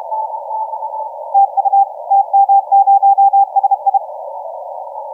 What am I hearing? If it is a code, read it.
Morse code, heard as a keyed tone, about 30 wpm, 790 Hz, with radio noise. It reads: TU TM0SI
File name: TM0SI 0722UTC 17m